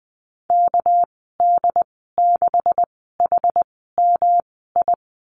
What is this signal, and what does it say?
Morse code, heard as a keyed tone, about 19 wpm, spaced as for 14 wpm, 700 Hz. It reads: KD6HMI
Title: 231 Lesson 37g - Letter x - Call Signs 20wpm